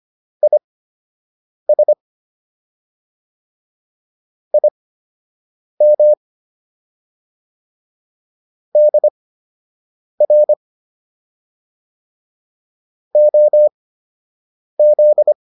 Morse code, heard as a keyed tone, about 25 wpm, spaced as for 7 wpm, 600 Hz. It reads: IS IM DR OZ